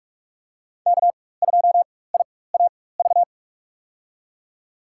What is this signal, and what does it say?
Morse code, heard as a keyed tone, about 46 wpm, spaced as for 21 wpm, 700 Hz. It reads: K2IAV
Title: USA Call Signs - 5 Characters - 4x Character Spacing - 45wpm